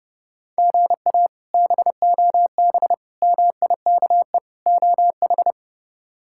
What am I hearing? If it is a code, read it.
Morse code, heard as a keyed tone, about 31 wpm, 700 Hz. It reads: GA BOB MIKE OH
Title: K1USN SST QSO Practice - 30wpm